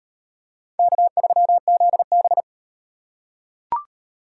Morse code, heard as a keyed tone, about 38 wpm, 700 Hz. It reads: K3ZB